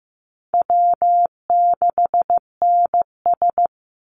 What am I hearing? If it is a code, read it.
Morse code, heard as a keyed tone, about 15 wpm, 700 Hz. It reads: W6NS
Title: Easy Call Signs Repeated 2x - 15wpm